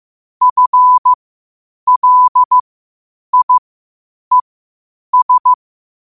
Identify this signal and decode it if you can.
Morse code, heard as a keyed tone, about 15 wpm, spaced as for 8 wpm, 1000 Hz. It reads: FLIES
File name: qotdfiles / 8wpm